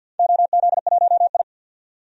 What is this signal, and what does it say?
Morse code, heard as a keyed tone, about 52 wpm, 700 Hz. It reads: OZ1I
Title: DXCC-100 Call Signs - 4 Characters - 50wpm